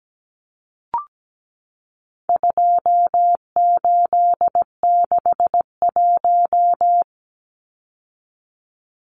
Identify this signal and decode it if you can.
Morse code, heard as a keyed tone, about 17 wpm, 700 Hz. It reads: 2861
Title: Numbers - 4-Digit 17wpm